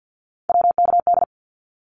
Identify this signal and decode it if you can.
Morse code, heard as a keyed tone, about 49 wpm, 700 Hz. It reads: WKD
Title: Abbreviations 50wpm